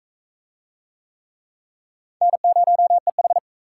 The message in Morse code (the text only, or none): N0EH